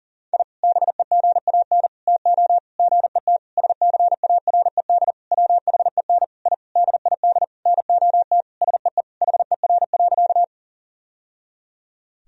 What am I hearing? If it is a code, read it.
Morse code, heard as a keyed tone, about 40 wpm, 700 Hz. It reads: I BEGAN TO GET SCARED WHEN I DID NOT SEE HER.
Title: Sentences from Top 1000 Words - Review Part 4 - 40wpm